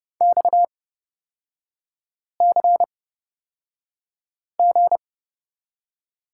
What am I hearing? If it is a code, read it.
Morse code, heard as a keyed tone, about 31 wpm, spaced as for 5 wpm, 690 Hz. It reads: XCG
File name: Lesson 37e - Letter x - Three Characters 30wpm